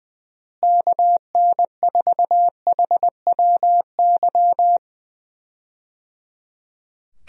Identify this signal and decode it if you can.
Morse code, heard as a keyed tone, about 21 wpm, 700 Hz. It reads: KN4HWY
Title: USA Call Signs - 6 Characters - Repeated 3x - 20wpm